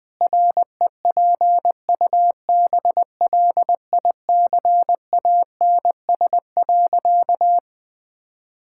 Morse code, heard as a keyed tone, about 21 wpm, 700 Hz. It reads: REPUBLICANS.
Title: Pangram Challenge v2 - 20wpm